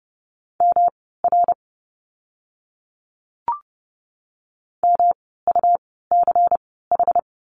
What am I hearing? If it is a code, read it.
Morse code, heard as a keyed tone, about 30 wpm, spaced as for 17 wpm, 700 Hz. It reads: MR MUCH
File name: course / 30:10wpm / 108 Lesson 18f - Letter m - Words 30wpm